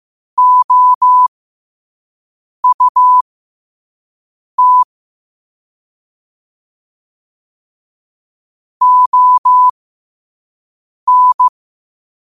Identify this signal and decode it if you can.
Morse code, heard as a keyed tone, about 15 wpm, spaced as for 5 wpm, 1000 Hz. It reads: OUT ON
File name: qotdfiles / 5wpm